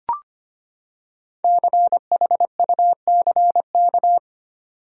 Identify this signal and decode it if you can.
Morse code, heard as a keyed tone, about 25 wpm, 700 Hz. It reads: CHUCK
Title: USA Common Names - Max 5 Letters - 25wpm